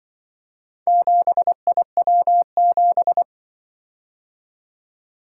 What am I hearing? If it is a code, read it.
Morse code, heard as a keyed tone, about 24 wpm, 700 Hz. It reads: 7IW7
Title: Speed-Racing - Four Letter-Number Combinations - 20wpm